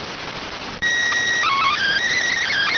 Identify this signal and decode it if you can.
SSTV (slow-scan television), PD 290.